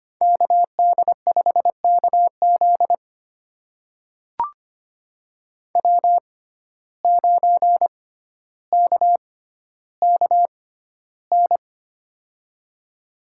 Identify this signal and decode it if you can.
Morse code, heard as a keyed tone, about 25 wpm, spaced as for 13 wpm, 700 Hz. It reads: KD5KZ W9KKN
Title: USA Call Signs - 5 Characters - 6x Character Spacing - 25wpm